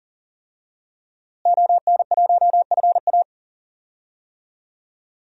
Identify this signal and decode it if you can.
Morse code, heard as a keyed tone, about 39 wpm, 700 Hz. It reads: ON1FA
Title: DXCC-100 Call Signs - 5 Characters - Repeated 2x - 40wpm